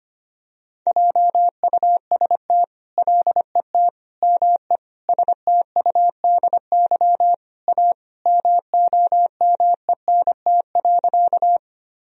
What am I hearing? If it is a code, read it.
Morse code, heard as a keyed tone, about 25 wpm, 700 Hz. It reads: JUST LET ME STUDY A MOMENT.
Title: Sentences from Top 1000 Words - Review Part 1 - 25wpm